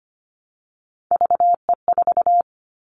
Morse code, heard as a keyed tone, about 25 wpm, 700 Hz. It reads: VE4
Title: Callsign Prefixes for Top 50 DXCC - 25wpm